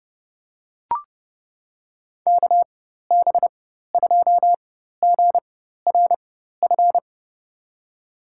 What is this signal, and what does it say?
Morse code, heard as a keyed tone, about 30 wpm, spaced as for 14 wpm, 700 Hz. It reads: KB2GRF
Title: USA Call Signs - 6 Characters - 4x Character Spacing - 30wpm